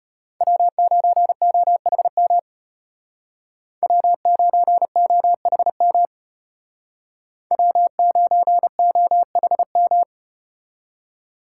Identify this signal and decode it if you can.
Morse code, heard as a keyed tone, about 33 wpm, 700 Hz. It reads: W9OHM W9OHM W9OHM